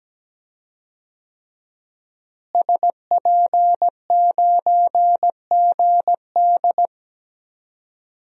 Morse code, heard as a keyed tone, about 17 wpm, 700 Hz. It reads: SP9GD
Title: DXCC-100 Call Signs - 5 Characters - Repeated 3x - 17wpm